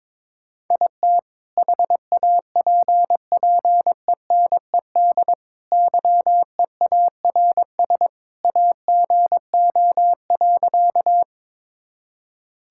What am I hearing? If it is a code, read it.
Morse code, heard as a keyed tone, about 22 wpm, 700 Hz. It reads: IT HAPPENED YEARS AGO.
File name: practice / 22wpm / Sentences with 2-Word Phrases - 22wpm